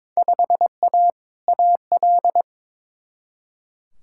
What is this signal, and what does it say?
Morse code, heard as a keyed tone, about 22 wpm, 700 Hz. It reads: 5A AL
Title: Field Day Exchanges Encore - 22wpm